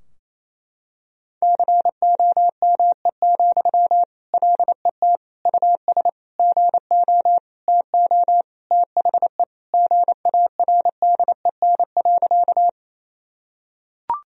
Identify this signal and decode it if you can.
Morse code, heard as a keyed tone, about 28 wpm, 700 Hz. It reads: COME, LET US GO TO THE GARDEN.
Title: Sentences from Top 1000 Words - Review Part 4 - 28wpm